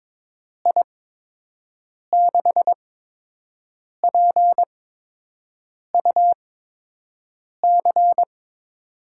Morse code, heard as a keyed tone, about 22 wpm, spaced as for 6 wpm, 700 Hz. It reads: I6PUC
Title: USA Call Signs - 6 Characters - 8x Character Spacing - 22wpm